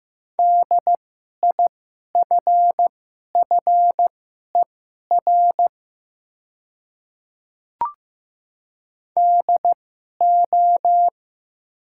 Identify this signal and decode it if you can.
Morse code, heard as a keyed tone, about 15 wpm, spaced as for 11 wpm, 700 Hz. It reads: DIFFER DO
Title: 2x Character Spacing - Top 300 Words - 15wpm